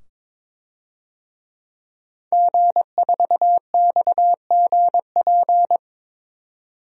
Morse code, heard as a keyed tone, about 22 wpm, 700 Hz. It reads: G4XGP
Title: DXCC-100 Call Signs - 5 Characters - 22wpm